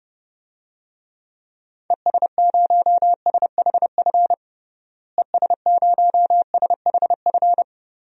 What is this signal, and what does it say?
Morse code, heard as a keyed tone, about 32 wpm, 700 Hz. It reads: ES0SHF ES0SHF
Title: DXCC-100 Call Signs - 6 Characters - Repeated 3x - 30wpm